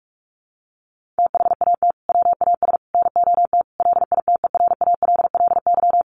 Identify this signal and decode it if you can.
Morse code, heard as a keyed tone, about 44 wpm, 700 Hz. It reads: THAT WAS NOT LITERALLY
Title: Sentences from Top 1400 Words 45wpm